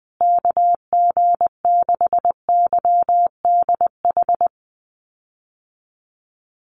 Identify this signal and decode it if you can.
Morse code, heard as a keyed tone, about 20 wpm, 700 Hz. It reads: KG6YDH
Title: USA Call Signs Repeated 3x 20wpm